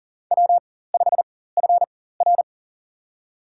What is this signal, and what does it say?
Morse code, heard as a keyed tone, about 40 wpm, spaced as for 19 wpm, 700 Hz. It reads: W5FR
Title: USA Call Signs - 4 Characters - 4x Character Spacing - 40wpm